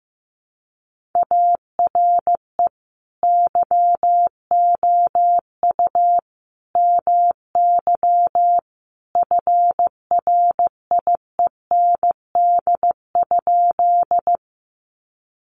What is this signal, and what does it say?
Morse code, heard as a keyed tone, about 15 wpm, 700 Hz. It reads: ARE YOU MY FRIEND?
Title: Sentences from Top 300 Words 15wpm